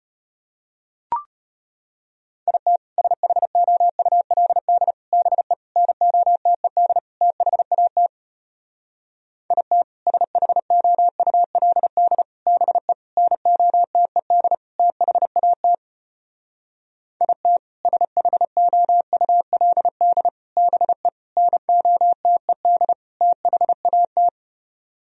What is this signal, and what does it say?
Morse code, heard as a keyed tone, about 33 wpm, 700 Hz. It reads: IT SHOULD BE NOTED THAT IT SHOULD BE NOTED THAT IT SHOULD BE NOTED THAT